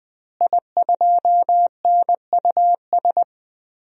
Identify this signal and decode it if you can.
Morse code, heard as a keyed tone, about 20 wpm, 700 Hz. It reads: I2NUS
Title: DXCC-100 Call Signs - 5 Characters - Repeated 2x - 20wpm